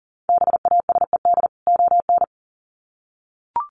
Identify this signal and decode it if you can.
Morse code, heard as a keyed tone, about 40 wpm, 700 Hz. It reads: BASED ON
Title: Sets of 2 Words 40wpm